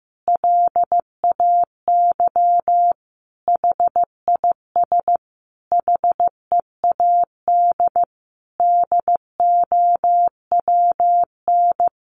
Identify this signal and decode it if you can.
Morse code, heard as a keyed tone, about 14 wpm, 700 Hz. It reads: LAY HIS HEAD DOWN
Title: Sentences from Top 1000 Words 15wpm